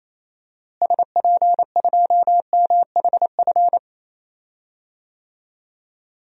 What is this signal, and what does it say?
Morse code, heard as a keyed tone, about 28 wpm, 700 Hz. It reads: SP2MHF
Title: DXCC-100 Call Signs - 6 Characters - Repeated 2x - 28wpm